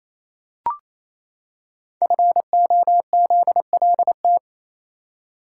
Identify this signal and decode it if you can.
Morse code, heard as a keyed tone, about 28 wpm, 700 Hz. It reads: FOZLT
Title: Code Groups - 5 Letters - 28wpm